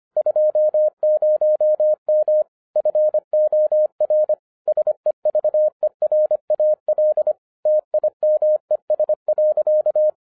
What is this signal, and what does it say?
Morse code, heard as a keyed tone, about 26 wpm, 610 Hz. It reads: 20M FOR SEVERAL TIMES.
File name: 2011-10-mp3 / n04-JUST MOVED 25wpm